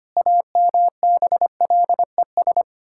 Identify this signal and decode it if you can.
Morse code, heard as a keyed tone, about 25 wpm, 700 Hz. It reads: AMBLES
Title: Pangram Challenge v2 - 25wpm